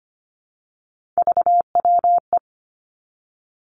Morse code, heard as a keyed tone, about 25 wpm, 700 Hz. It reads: VWE
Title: Three Challenging Letter-Number Combinations 25wpm